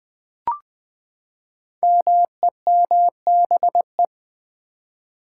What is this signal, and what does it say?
Morse code, heard as a keyed tone, about 20 wpm, 700 Hz. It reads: MEMBE
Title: Build A Word - 5 Letters At A Time - 10-Letter Words - 20wpm